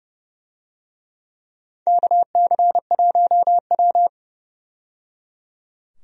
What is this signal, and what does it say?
Morse code, heard as a keyed tone, about 30 wpm, 700 Hz. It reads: KC1W
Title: USA Call Signs - 4 Characters - 30wpm